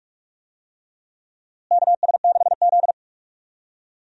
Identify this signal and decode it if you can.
Morse code, heard as a keyed tone, about 46 wpm, 700 Hz. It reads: KS6Z
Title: USA Call Signs - 4 Characters - Repeated 2x - 45wpm